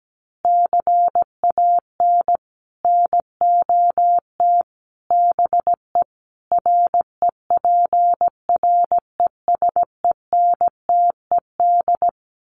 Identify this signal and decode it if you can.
Morse code, heard as a keyed tone, about 17 wpm, 700 Hz. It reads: CAN NOT BE REPRESENTED
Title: Sets of 5 Words 17wpm